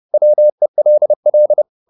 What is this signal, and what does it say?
Morse code, heard as a keyed tone, about 30 wpm, 600 Hz. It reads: WELL